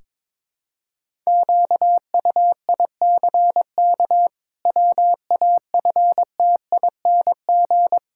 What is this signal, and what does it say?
Morse code, heard as a keyed tone, about 22 wpm, 700 Hz. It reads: QUICK WAFTING
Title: Pangram Challenge v2 - 22wpm